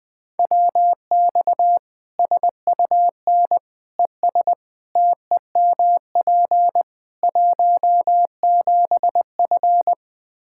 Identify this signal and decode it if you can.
Morse code, heard as a keyed tone, about 21 wpm, 700 Hz. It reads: WX SUN ES TEMP 17F
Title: K1USN SST QSO Practice 20wpm